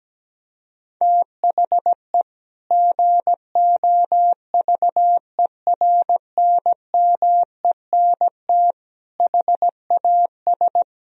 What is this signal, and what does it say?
Morse code, heard as a keyed tone, about 17 wpm, 700 Hz. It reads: THE GOVERNMENT HAS
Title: Sentences from Top 1000 Words - Review Part 3 - 17wpm